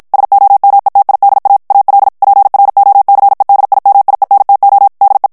Morse code, heard as a keyed tone, about 56 wpm, 800 Hz. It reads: SOMETIDA AL PROCEDIMIENTO DE
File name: morse mp3 / ESPAGNOL TEXTE EN CLAIR 2 VITESSE 50 MOTS MINUTES